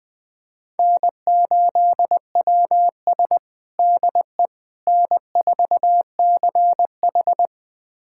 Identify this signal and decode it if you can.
Morse code, heard as a keyed tone, about 20 wpm, 700 Hz. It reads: N8WS DE N4CH
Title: USA Callsign DE Callsign - 4 Characters - 20wpm